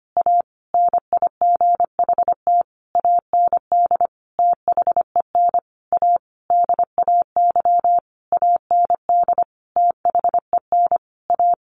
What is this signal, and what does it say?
Morse code, heard as a keyed tone, about 25 wpm, 700 Hz. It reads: A NIGHT AND THEN A DAY AND THEN A
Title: A Sound Of Thunder 25wpm